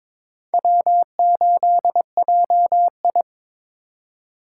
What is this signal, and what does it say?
Morse code, heard as a keyed tone, about 22 wpm, 700 Hz. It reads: W8JI
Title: USA Call Signs - 4 Characters - 3x Character Spacing - 22wpm